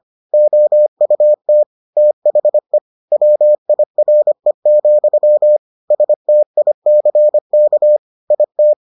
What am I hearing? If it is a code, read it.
Morse code, heard as a keyed tone, about 25 wpm, 600 Hz. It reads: OUT THE WIRE, STICK IT